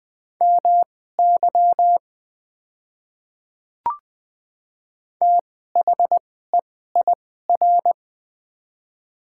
Morse code, heard as a keyed tone, about 21 wpm, spaced as for 15 wpm, 700 Hz. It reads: MY THEIR
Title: 2x Character Spacing - Top 100 Words - 20wpm